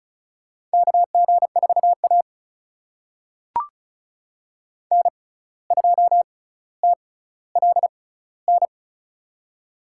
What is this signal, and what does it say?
Morse code, heard as a keyed tone, about 34 wpm, spaced as for 17 wpm, 700 Hz. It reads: KG4A N2TLN
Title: USA Call Signs - 6x Character Spacing - 35wpm